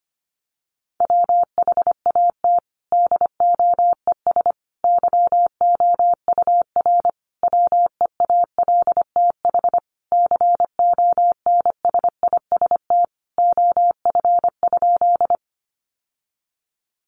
Morse code, heard as a keyed tone, about 25 wpm, 700 Hz. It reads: WHAT DOES YOUR WEALTH CONSIST OF?
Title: Sentences from Top 1400 Words 25wpm